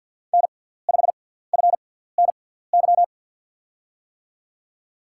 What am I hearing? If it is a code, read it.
Morse code, heard as a keyed tone, about 50 wpm, spaced as for 17 wpm, 700 Hz. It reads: N5FNY
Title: USA Call Signs - 5 Characters - 6x Character Spacing - 50wpm